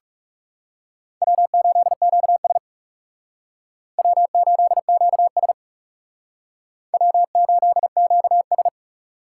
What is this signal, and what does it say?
Morse code, heard as a keyed tone, about 39 wpm, 700 Hz. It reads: W8QS W8QS W8QS